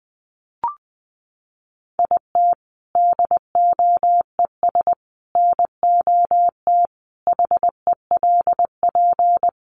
Morse code, heard as a keyed tone, about 20 wpm, 700 Hz. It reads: IT DOES NOT HELP
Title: Sentences from Top 1500 Words 20wpm